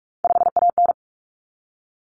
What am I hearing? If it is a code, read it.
Morse code, heard as a keyed tone, about 44 wpm, 700 Hz. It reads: 5AN
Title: Common Cut Numbers - 3-Digit 45wpm